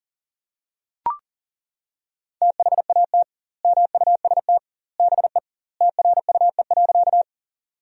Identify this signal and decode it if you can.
Morse code, heard as a keyed tone, about 40 wpm, 700 Hz. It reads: THAT MUST BE TRUE.
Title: 2x Word-Spacing - Sentences from Top 400 Words - 40wpm